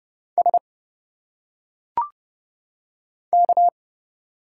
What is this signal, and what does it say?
Morse code, heard as a keyed tone, about 30 wpm, 700 Hz. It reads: S K
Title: Single Letters - Phonetic Pronunciation 30wpm